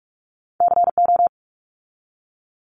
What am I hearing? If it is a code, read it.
Morse code, heard as a keyed tone, about 46 wpm, 700 Hz. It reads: CO
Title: ARRL Sections 45wpm